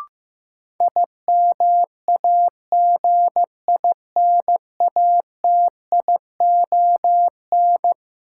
Morse code, heard as a keyed tone, about 15 wpm, 700 Hz. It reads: IMAGINATION
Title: Sentences from Top 1300 Words - New Word Drill - 15wpm